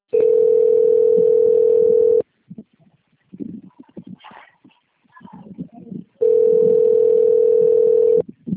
Ringback tone (North America), 440 + 480 Hz.